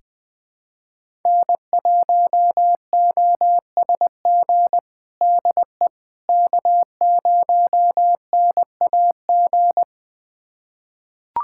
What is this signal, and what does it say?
Morse code, heard as a keyed tone, about 21 wpm, 700 Hz. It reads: N1OSG DE K0NAG
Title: USA Callsign DE Callsign - 5 Characters - 20wpm